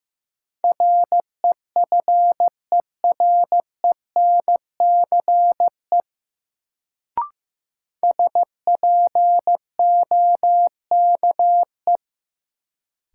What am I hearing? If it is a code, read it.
Morse code, heard as a keyed tone, about 15 wpm, 700 Hz. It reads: REFERENCE SPOKE
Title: Sentences from Top 1200 Words - New Word Drill - 15wpm